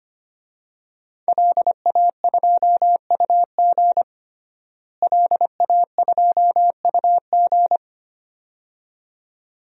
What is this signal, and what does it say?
Morse code, heard as a keyed tone, about 25 wpm, 700 Hz. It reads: LA2UG LA2UG
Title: DXCC-100 Call Signs - 5 Characters - Repeated 2x - 25wpm